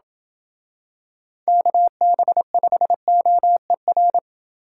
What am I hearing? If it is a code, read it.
Morse code, heard as a keyed tone, about 26 wpm, 700 Hz. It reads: KB5OER